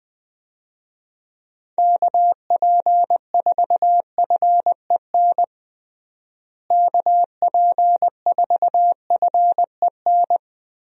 Morse code, heard as a keyed tone, about 20 wpm, 700 Hz. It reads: KP4FEN KP4FEN
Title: DXCC-100 Call Signs - 6 Characters - Repeated 2x - 20wpm